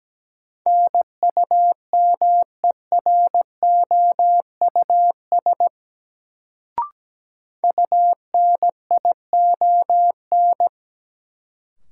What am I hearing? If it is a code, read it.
Morse code, heard as a keyed tone, about 17 wpm, 700 Hz. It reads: NUMEROUS UNION
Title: Sentences from Top 1400 Words - New Word Drill Alt - 17wpm